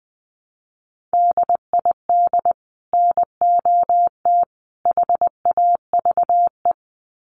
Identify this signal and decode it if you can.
Morse code, heard as a keyed tone, about 20 wpm, 700 Hz. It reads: DID NOT HAVE
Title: Sets of 3 Words 20wpm